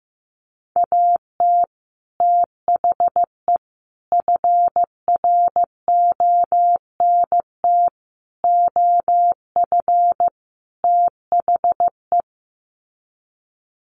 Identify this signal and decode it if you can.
Morse code, heard as a keyed tone, about 14 wpm, 700 Hz. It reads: AT THE FRONT OF THE
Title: Sets of 5 Words 15wpm